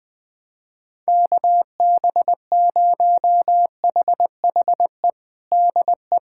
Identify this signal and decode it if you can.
Morse code, heard as a keyed tone, about 20 wpm, 700 Hz. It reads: KB0HHE DE
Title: USA Callsign DE Callsign 20wpm